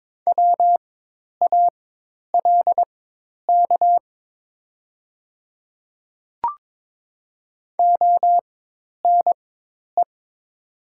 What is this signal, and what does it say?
Morse code, heard as a keyed tone, about 22 wpm, spaced as for 10 wpm, 700 Hz. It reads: WALK ONE
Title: 4x Character Spacing - Top 300 Words - 22wpm